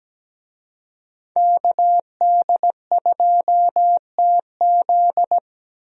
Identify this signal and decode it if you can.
Morse code, heard as a keyed tone, about 17 wpm, 700 Hz. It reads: KD2TZ
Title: USA Call Signs - 5 Characters - Repeated 3x - 17wpm